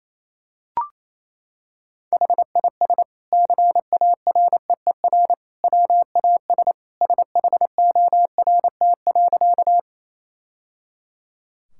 Morse code, heard as a keyed tone, about 28 wpm, 700 Hz. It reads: HIS CAREER WAS SHORT.
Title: Sentences from Top 1000 Words - Review Part 6 - 28wpm